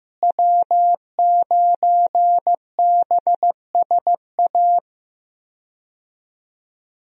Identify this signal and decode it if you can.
Morse code, heard as a keyed tone, about 15 wpm, 700 Hz. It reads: W9BSA
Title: USA Call Signs - 5 Characters - Repeated 2x - 15wpm